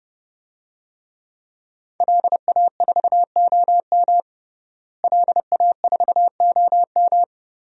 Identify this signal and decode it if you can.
Morse code, heard as a keyed tone, about 30 wpm, 700 Hz. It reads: LA4OM LA4OM
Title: DXCC-100 Call Signs - 5 Characters - Repeated 2x - 30wpm